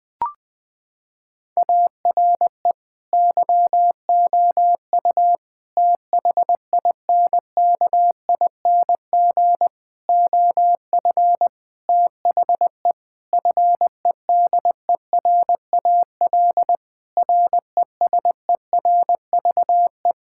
Morse code, heard as a keyed tone, about 20 wpm, 700 Hz. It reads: ARE YOU THINKING OF THE FEDERAL RESERVE
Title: Sentences from Top 1000 Words - Review Part 6 - 20wpm